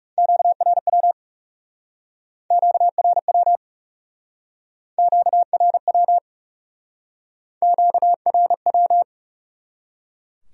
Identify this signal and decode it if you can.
Morse code, heard as a keyed tone, about 39 wpm, 700 Hz. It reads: QRW QRW QRW QRW